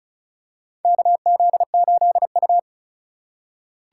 Morse code, heard as a keyed tone, about 36 wpm, 700 Hz. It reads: KZ8U
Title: USA Call Signs - 4 Characters - 3x Character Spacing - 35wpm